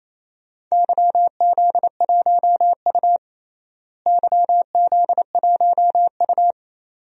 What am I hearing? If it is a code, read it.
Morse code, heard as a keyed tone, about 28 wpm, 700 Hz. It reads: YZ1U YZ1U
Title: DXCC-100 Call Signs - 4 Characters - Repeated 2x - 28wpm